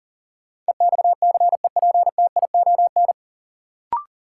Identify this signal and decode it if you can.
Morse code, heard as a keyed tone, about 40 wpm, 700 Hz. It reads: EXCEPTION
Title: Sets of 4 Words - New Word Drill - 40wpm